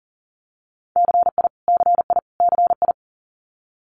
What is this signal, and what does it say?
Morse code, heard as a keyed tone, about 43 wpm, 700 Hz. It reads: CI CI CI
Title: Multi-letter Phonemes Repeated 3x 40wpm